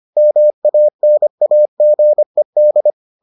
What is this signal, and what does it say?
Morse code, heard as a keyed tone, about 25 wpm, 600 Hz. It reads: MANAGED